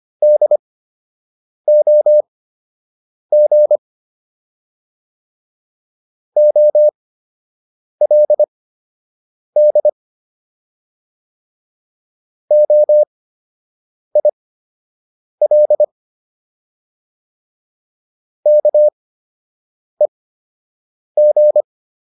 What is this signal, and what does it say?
Morse code, heard as a keyed tone, about 25 wpm, spaced as for 7 wpm, 600 Hz. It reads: DOG OLD OIL KEG